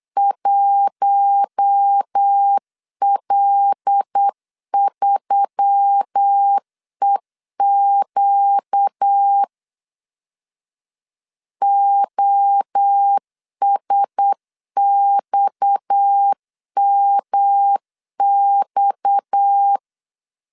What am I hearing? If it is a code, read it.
Morse code, heard as a keyed tone, about 8 wpm, 800 Hz. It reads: 1L3EQ OSXMX
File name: LEZIONI / 6 PKX0 / 6 PKX 8wpm 3041